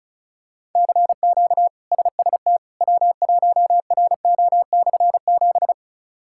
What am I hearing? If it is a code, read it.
Morse code, heard as a keyed tone, about 34 wpm, 700 Hz. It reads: CQ SST W1RO/7
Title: K1USN SST QSO Practice - 35wpm